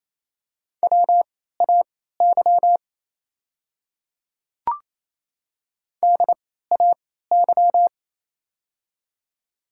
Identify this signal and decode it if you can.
Morse code, heard as a keyed tone, about 29 wpm, spaced as for 16 wpm, 690 Hz. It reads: WAY DAY
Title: 3x Character Spacing - Top 100 Words - 28wpm